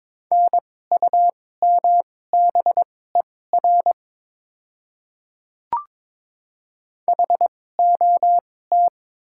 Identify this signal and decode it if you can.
Morse code, heard as a keyed tone, about 22 wpm, spaced as for 16 wpm, 700 Hz. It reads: NUMBER HOT
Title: 2x Character Spacing - Top 100 Words - 22wpm